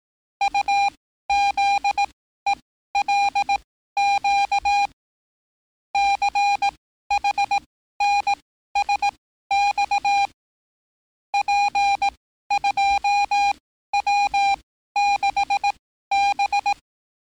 Morse code, heard as a keyed tone, about 18 wpm, spaced as for 13 wpm, 810 Hz. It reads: UZELQ CHNSX P2W6B
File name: training / NAVPERS 11049to11060 / NAVPERS 11054 RA RB